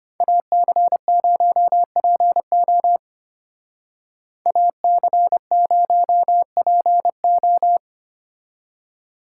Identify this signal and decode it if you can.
Morse code, heard as a keyed tone, about 33 wpm, 700 Hz. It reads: AC0PO AC0PO